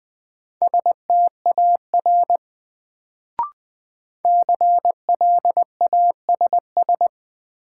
Morse code, heard as a keyed tone, about 20 wpm, 700 Hz. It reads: STAR CLASS
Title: Sentences from Top 1000 Words - Review Part 2 - New Word Drill - 20wpm